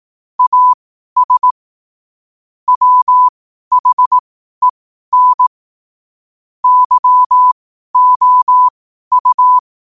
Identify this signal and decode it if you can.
Morse code, heard as a keyed tone, about 18 wpm, spaced as for 12 wpm, 1000 Hz. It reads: AS WHEN YOU
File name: qotdfiles / 12wpm